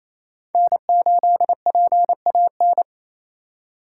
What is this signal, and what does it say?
Morse code, heard as a keyed tone, about 28 wpm, 700 Hz. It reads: N8PAN
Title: USA Call Signs - 5 Characters - 4x Character Spacing - 28wpm